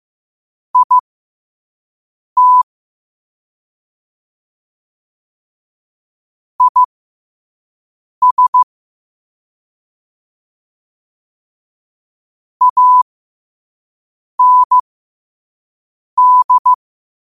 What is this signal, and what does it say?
Morse code, heard as a keyed tone, about 15 wpm, spaced as for 5 wpm, 1000 Hz. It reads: IT IS AND
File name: qotdfiles / 5wpm